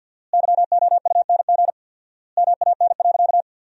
Morse code, heard as a keyed tone, about 50 wpm, 700 Hz. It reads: YOUNG MAN.
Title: 4x Word-Spacing - Sentences from Top 400 Words - 50wpm